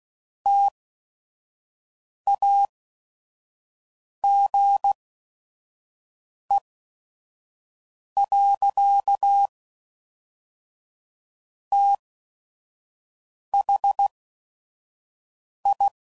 Morse code, heard as a keyed tone, about 16 wpm, 800 Hz. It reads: T A G E . T H I